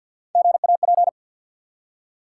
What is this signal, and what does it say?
Morse code, heard as a keyed tone, about 48 wpm, 700 Hz. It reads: GAP
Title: Three Letter Words - 50wpm